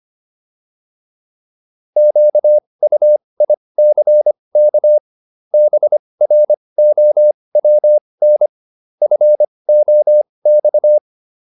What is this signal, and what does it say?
Morse code, heard as a keyed tone, about 25 wpm, spaced as for 20 wpm, 600 Hz. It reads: QUICK BROWN FOX